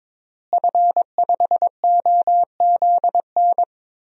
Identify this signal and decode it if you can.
Morse code, heard as a keyed tone, about 22 wpm, 700 Hz. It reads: F5OZN
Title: DXCC-100 Call Signs - 5 Characters - Repeated 3x - 22wpm